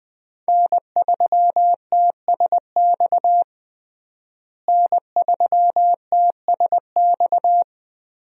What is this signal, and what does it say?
Morse code, heard as a keyed tone, about 21 wpm, 700 Hz. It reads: N3TSX N3TSX
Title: USA Call Signs - 5 Characters - Repeated 2x - 20wpm